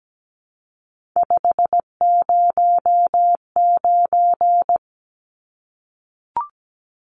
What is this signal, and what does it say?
Morse code, heard as a keyed tone, about 17 wpm, 700 Hz. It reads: 509
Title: Numbers - 3-Digit 17wpm